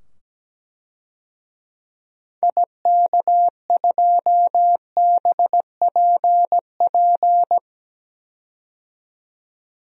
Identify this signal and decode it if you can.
Morse code, heard as a keyed tone, about 17 wpm, 700 Hz. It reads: IK2BPP
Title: DXCC-100 Call Signs - 6 Characters - 17wpm